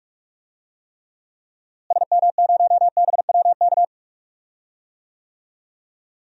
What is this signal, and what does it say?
Morse code, heard as a keyed tone, about 42 wpm, 700 Hz. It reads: SM0BWK
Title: DXCC-100 Call Signs - 6 Characters - Repeated 3x - 45wpm